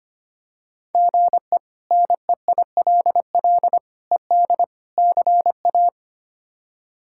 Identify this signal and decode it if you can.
Morse code, heard as a keyed tone, about 25 wpm, 700 Hz. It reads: GE NEILL ED CA
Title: K1USN SST QSO Practice - 25wpm